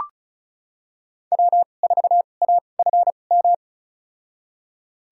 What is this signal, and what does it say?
Morse code, heard as a keyed tone, about 36 wpm, spaced as for 25 wpm, 700 Hz. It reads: W4AFM
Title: USA Call Signs - 5 Characters - 2x Character Spacing - 35wpm